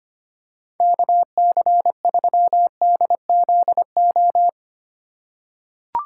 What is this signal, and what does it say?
Morse code, heard as a keyed tone, about 25 wpm, 700 Hz. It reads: KC3DZO
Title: USA Call Signs - 6 Characters - 3x Character Spacing - 25wpm